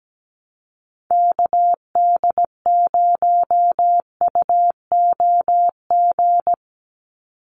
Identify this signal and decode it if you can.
Morse code, heard as a keyed tone, about 17 wpm, 700 Hz. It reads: KD0UOG
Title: USA Call Signs 17wpm